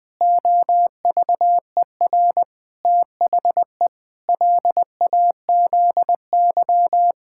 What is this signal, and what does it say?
Morse code, heard as a keyed tone, about 20 wpm, 700 Hz. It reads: OVER THE LAZY
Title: Pangram Challenge v2 - 20wpm